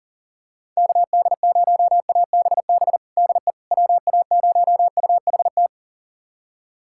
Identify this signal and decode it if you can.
Morse code, heard as a keyed tone, about 40 wpm, 700 Hz. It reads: KD0ABB DE WA0UHT
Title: USA Callsign DE Callsign - 6 Characters - 40wpm